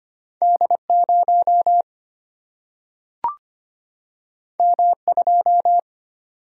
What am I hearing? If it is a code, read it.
Morse code, heard as a keyed tone, about 25 wpm, 700 Hz. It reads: D0 M2
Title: Single-Letter Single-Number 25wpm